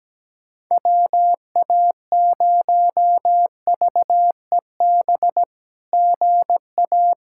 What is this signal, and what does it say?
Morse code, heard as a keyed tone, about 17 wpm, 700 Hz. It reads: WA0VEB GA
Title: SOTA QSO Practice 17wpm